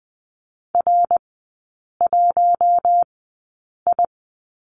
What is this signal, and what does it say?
Morse code, heard as a keyed tone, about 20 wpm, spaced as for 8 wpm, 700 Hz. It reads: R1I
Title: 053 Lesson 10e - Letter r - Three Characters 20wpm